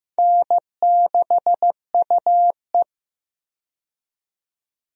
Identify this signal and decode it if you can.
Morse code, heard as a keyed tone, about 15 wpm, 700 Hz. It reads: N6UE
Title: USA Call Signs - 4 Characters - 15wpm